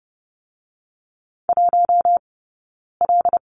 Morse code, heard as a keyed tone, about 30 wpm, 700 Hz. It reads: 1 L
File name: Lesson 16d - Letter c - Two Characters 30wpm